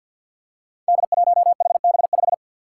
Morse code, heard as a keyed tone, about 47 wpm, 700 Hz. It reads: D1HB5
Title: Five Challenging Letter-Number Combinations 50wpm